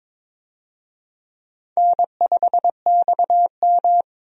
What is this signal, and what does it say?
Morse code, heard as a keyed tone, about 22 wpm, 700 Hz. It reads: N5XM
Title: USA Call Signs - 4 Characters - 22wpm